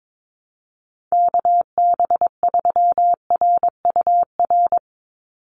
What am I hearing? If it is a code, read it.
Morse code, heard as a keyed tone, about 22 wpm, 700 Hz. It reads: KB3RUR
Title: USA Call Signs 22wpm